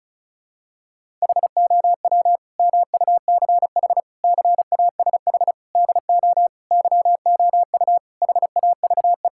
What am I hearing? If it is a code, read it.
Morse code, heard as a keyed tone, about 36 wpm, 700 Hz. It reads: HOW MUCH CASH DO YOU HAVE
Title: Sentences from Top 1000 Words - Review Part 5 - 35wpm